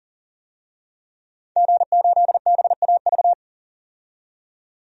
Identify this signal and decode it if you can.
Morse code, heard as a keyed tone, about 40 wpm, 700 Hz. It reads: G8BAV
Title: DXCC-100 Call Signs - 5 Characters - 40wpm